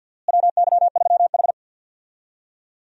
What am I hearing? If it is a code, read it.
Morse code, heard as a keyed tone, about 52 wpm, 700 Hz. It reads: WY3H
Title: USA Call Signs - 4 Characters - Repeated 2x - 50wpm